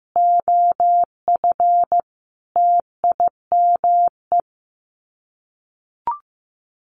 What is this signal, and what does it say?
Morse code, heard as a keyed tone, about 15 wpm, 700 Hz. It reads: OF TIME
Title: Sets of 5 Words 15wpm